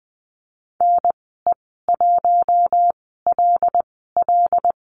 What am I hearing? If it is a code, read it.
Morse code, heard as a keyed tone, about 20 wpm, spaced as for 15 wpm, 700 Hz. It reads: NE1LL
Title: 082 Lesson 14g - Number 2 - Call Signs 20wpm